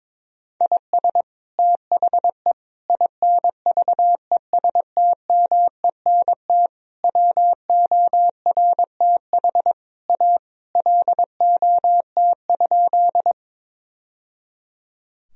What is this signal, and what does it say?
Morse code, heard as a keyed tone, about 21 wpm, 700 Hz. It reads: IS THE INVESTMENT WORTH A LOT?
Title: Sentences from Top 1000 Words - Review Part 4 - 22wpm